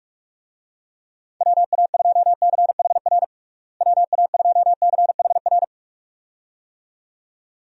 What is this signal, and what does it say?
Morse code, heard as a keyed tone, about 46 wpm, 700 Hz. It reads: WA2CHR WA2CHR
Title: USA Call Signs - 6 Characters - Repeated 2x - 45wpm